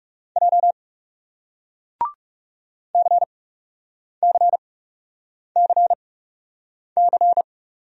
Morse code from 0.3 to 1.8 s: J

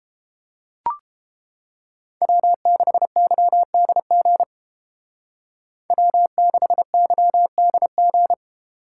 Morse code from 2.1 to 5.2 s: W6YDG

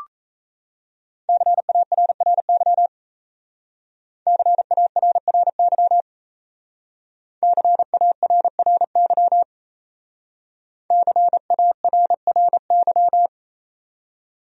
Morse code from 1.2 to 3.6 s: CARRY

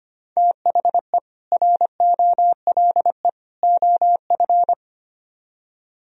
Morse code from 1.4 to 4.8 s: ROLE OF

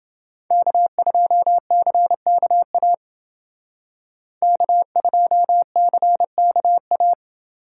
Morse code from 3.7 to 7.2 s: K2CKA

